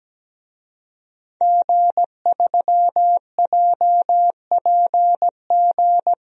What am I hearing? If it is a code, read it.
Morse code, heard as a keyed tone, about 17 wpm, 700 Hz. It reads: G3JPG
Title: DXCC-100 Call Signs - 5 Characters - Repeated 2x - 17wpm